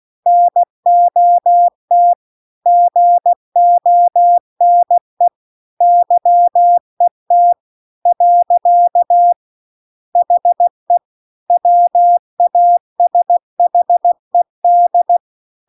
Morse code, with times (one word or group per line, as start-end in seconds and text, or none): 0.3-2.1 NOT
2.7-5.3 GONE
5.8-7.5 YET
8.1-9.3 .
10.2-11.0 HE
11.5-15.2 WASHED